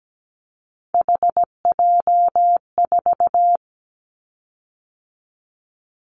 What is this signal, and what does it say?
Morse code, heard as a keyed tone, about 17 wpm, 700 Hz. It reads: HJ4
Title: Callsign Prefixes for Top 100 DXCC - 17wpm